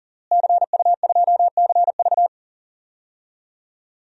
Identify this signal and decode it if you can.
Morse code, heard as a keyed tone, about 40 wpm, 700 Hz. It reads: CU2CV
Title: DXCC-100 Call Signs - 5 Characters - Repeated 3x - 40wpm